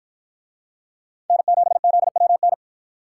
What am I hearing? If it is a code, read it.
Morse code, heard as a keyed tone, about 51 wpm, 700 Hz. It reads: N7ZWN